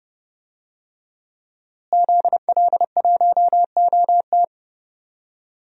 Morse code, heard as a keyed tone, about 30 wpm, 700 Hz. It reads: ZL1OT
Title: DXCC-100 Call Signs - 5 Characters - Repeated 2x - 30wpm